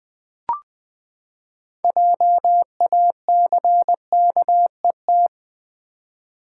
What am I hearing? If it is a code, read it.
Morse code, heard as a keyed tone, about 21 wpm, 700 Hz. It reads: JACKET
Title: Six Letter Words - 20wpm